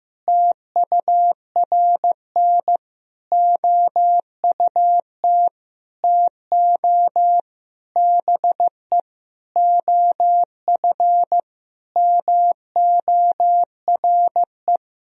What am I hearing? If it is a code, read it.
Morse code, heard as a keyed tone, about 15 wpm, 700 Hz. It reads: TURN OUT TO BE OF MORE
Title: Sentences from Top 400 Words and Abbreviations 15wpm